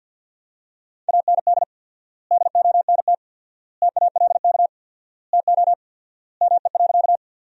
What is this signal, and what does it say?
Morse code, heard as a keyed tone, about 50 wpm, 700 Hz. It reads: AND DONT TALK TO ME.
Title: 4x Word-Spacing - Sentences from Top 400 Words - 50wpm